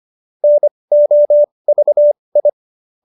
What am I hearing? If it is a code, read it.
Morse code, heard as a keyed tone, about 25 wpm, spaced as for 20 wpm, 590 Hz. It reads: NOVI